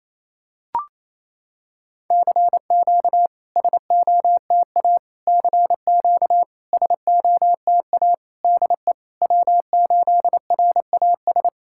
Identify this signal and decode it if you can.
Morse code, heard as a keyed tone, about 28 wpm, 700 Hz. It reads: CQ SOTA CQ SOTA DE W8RAS
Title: SOTA QSO Practice 28wpm